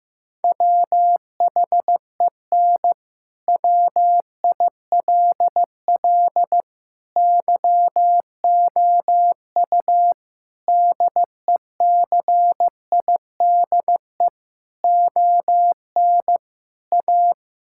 Morse code, heard as a keyed tone, about 15 wpm, 700 Hz. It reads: WHEN WILL YOU DECIDE ON A
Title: Sentences from Top 1000 Words - Review Part 6 - 15wpm